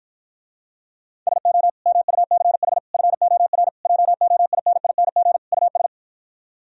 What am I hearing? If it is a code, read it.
Morse code, heard as a keyed tone, about 52 wpm, 700 Hz. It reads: SO MUCH FOR JOINING US